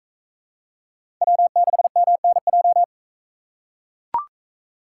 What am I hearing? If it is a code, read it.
Morse code, heard as a keyed tone, about 39 wpm, 700 Hz. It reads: W6MNJ